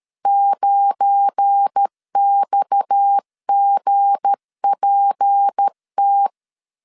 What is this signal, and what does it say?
Morse code, heard as a keyed tone, about 13 wpm, 800 Hz. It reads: 9XGPT
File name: CW / DET / 1 9 1500C S B 12wpm 3819